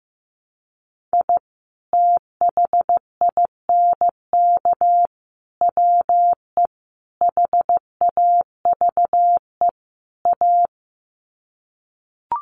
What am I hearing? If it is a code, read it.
Morse code, heard as a keyed tone, about 15 wpm, 700 Hz. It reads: I THINK WE HAVE A
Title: Sets of 5 Words Encore 15wpm